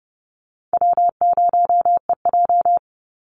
Morse code, heard as a keyed tone, about 30 wpm, 700 Hz. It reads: W0EJ
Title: Easy Call Signs Repeated 2x - 30wpm